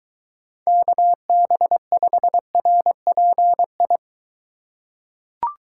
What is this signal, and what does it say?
Morse code, heard as a keyed tone, about 22 wpm, 700 Hz. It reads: KB5RPI